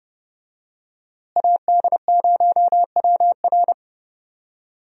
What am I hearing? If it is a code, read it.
Morse code, heard as a keyed tone, about 28 wpm, 700 Hz. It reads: AD0WR
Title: USA Call Signs - 5 Characters - Repeated 3x - 30wpm